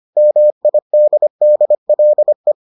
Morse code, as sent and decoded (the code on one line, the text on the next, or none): -- .. -.. -.. .-.. .
MIDDLE